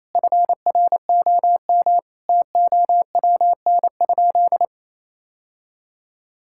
Morse code, heard as a keyed tone, about 28 wpm, 700 Hz. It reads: FROM TOWN?
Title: Sentences from Top 1000 Words - Review Part 5 - 28wpm